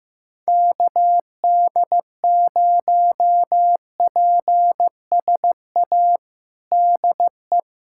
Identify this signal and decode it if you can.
Morse code, heard as a keyed tone, about 15 wpm, 700 Hz. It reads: KD0PSA DE
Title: USA Callsign DE Callsign - 6 Characters - 15wpm